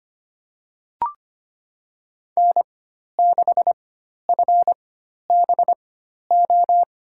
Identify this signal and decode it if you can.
Morse code, heard as a keyed tone, about 25 wpm, spaced as for 12 wpm, 700 Hz. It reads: N6FBO
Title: USA Call Signs - 4x Character Spacing - 25wpm